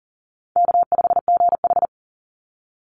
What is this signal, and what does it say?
Morse code, heard as a keyed tone, about 39 wpm, 700 Hz. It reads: K5GH
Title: Easy Call Signs Repeated 3x - 40wpm